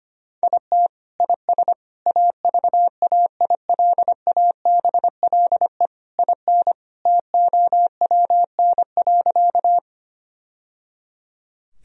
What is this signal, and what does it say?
Morse code, heard as a keyed tone, about 25 wpm, 700 Hz. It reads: IT IS AVAILABLE IN TOWN.
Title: Sentences from Top 1000 Words - Review Part 3 - 25wpm